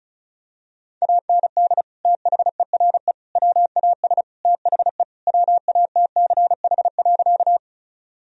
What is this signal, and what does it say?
Morse code, heard as a keyed tone, about 36 wpm, 700 Hz. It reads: AND THERE WAS THE WATCH.
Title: Sentences from Top 1000 Words - Review Part 1 - 35wpm